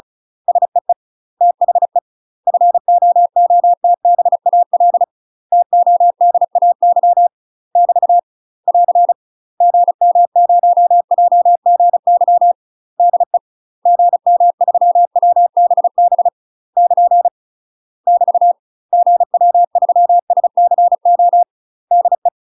Morse code, heard as a keyed tone, about 36 wpm, 700 Hz. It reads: SEE THE FOOTBAL TODAY <BT> <AR> GM0JGY DE GM3WBB <KN> <BT> GW3SCO DE
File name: UK MASTER QSOs TRAINING / 35WPM / UK MASTER QSOs 0035